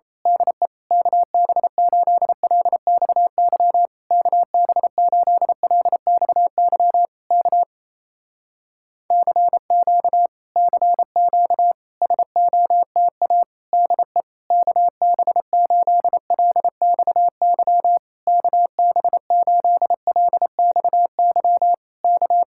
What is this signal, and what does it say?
Morse code, heard as a keyed tone, about 27 wpm, 700 Hz. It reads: DE KB8LXY KB8LXY K CQ CQ SOTA DE KB8LXY KB8LXY K
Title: Speed-Racing Common QSO Phrases Encore - 28wpm